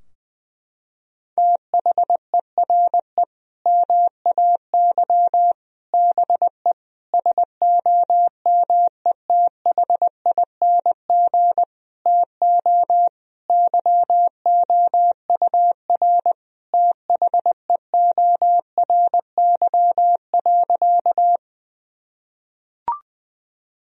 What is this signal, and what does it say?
Morse code, heard as a keyed tone, about 20 wpm, 700 Hz. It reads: THERE MAY BE SOMETHING TO YOUR THEORY.
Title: Sentences from Top 1000 Words - Review Part 3 - 20wpm